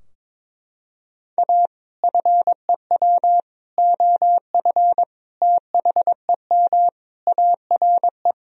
Morse code, heard as a keyed tone, about 22 wpm, 700 Hz. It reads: A FEW OF THEM ARE
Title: Sentences from Top 1000 Words - Review Part 3 - 22wpm